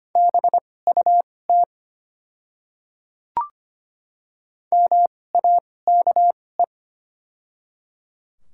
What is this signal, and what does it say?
Morse code, heard as a keyed tone, about 25 wpm, spaced as for 18 wpm, 700 Hz. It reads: BUT MAKE
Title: 2x Character Spacing - Top 200 Words - 25wpm